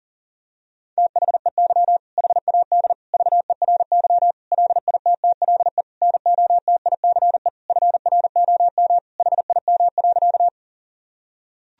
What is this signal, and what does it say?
Morse code, heard as a keyed tone, about 37 wpm, 700 Hz. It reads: THEY HAD VERY LITTLE NOTICE FROM HIM.
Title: Sentences from Top 1000 Words - Review Part 2 - 40wpm